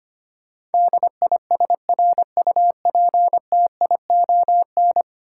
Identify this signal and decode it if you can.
Morse code, heard as a keyed tone, about 25 wpm, 690 Hz. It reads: DISRUPTION
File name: 10 Letter Words - First 3 Letters Given - 25wpm